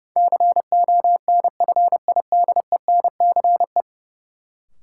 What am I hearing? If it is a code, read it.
Morse code, heard as a keyed tone, about 32 wpm, 700 Hz. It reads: CONFIDENCE
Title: Sentences from Top 1500 Words - New Word Drill Alt - 30wpm